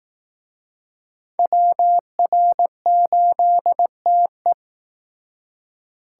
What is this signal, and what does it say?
Morse code, heard as a keyed tone, about 19 wpm, 700 Hz. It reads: WR8TE
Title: Speed Racing - USA 2x2 Call Signs 15wpm